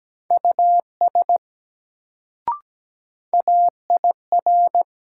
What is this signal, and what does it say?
Morse code, heard as a keyed tone, about 17 wpm, 700 Hz. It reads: US AIR
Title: Sets of 2 Words - New Word Drill - 17wpm